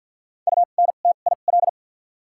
Morse code, heard as a keyed tone, about 53 wpm, spaced as for 37 wpm, 700 Hz. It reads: UNTIL
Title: 2x Character Spacing - Top 500 Words - 50wpm